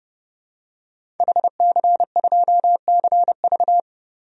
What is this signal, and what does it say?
Morse code, heard as a keyed tone, about 30 wpm, 700 Hz. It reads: HC2CV
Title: DXCC-100 Call Signs - 5 Characters - 30wpm